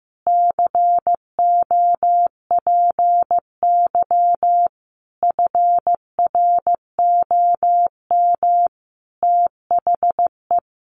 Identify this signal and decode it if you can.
Morse code, heard as a keyed tone, about 15 wpm, 700 Hz. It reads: COPY FROM THE
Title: Sentences from Top 1400 Words 15wpm